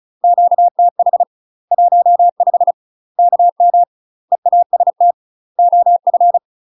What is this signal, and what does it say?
Morse code, heard as a keyed tone, about 34 wpm, 700 Hz. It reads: QTH 15 KM EAST OF
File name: UK MASTER QSOs TRAINING / 35WPM / UK MASTER QSOs 0025